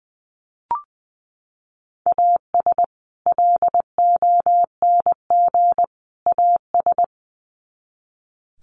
Morse code, heard as a keyed tone, about 20 wpm, 700 Hz. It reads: AS LONG AS
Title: Sets of 3 Words 20wpm